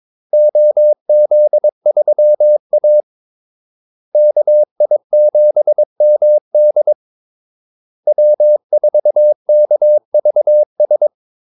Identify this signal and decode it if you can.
Morse code, heard as a keyed tone, about 22 wpm, 600 Hz. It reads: OZ3A KI7MD W4KVS
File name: Callsigns-22wpm